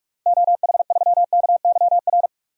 Morse code, heard as a keyed tone, about 46 wpm, 700 Hz. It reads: OH3KYR
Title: DXCC-100 Call Signs - 6 Characters - 45wpm